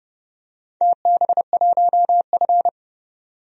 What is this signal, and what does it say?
Morse code, heard as a keyed tone, about 30 wpm, 700 Hz. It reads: TB1F
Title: DXCC-100 Call Signs - 4 Characters - 30wpm